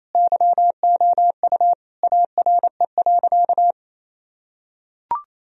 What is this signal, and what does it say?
Morse code, heard as a keyed tone, about 28 wpm, 700 Hz. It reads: YOU ARE.
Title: Sentences from Top 1000 Words - Review Part 1 - 28wpm